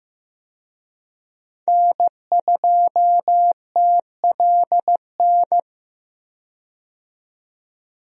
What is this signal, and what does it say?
Morse code, heard as a keyed tone, about 15 wpm, 700 Hz. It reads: N2TLN
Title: USA Call Signs - 5 Characters - 15wpm